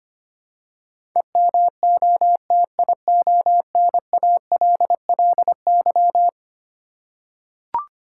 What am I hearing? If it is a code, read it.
Morse code, heard as a keyed tone, about 25 wpm, 700 Hz. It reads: EMOTIONALLY
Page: morse-code-ninja-practice-25wpm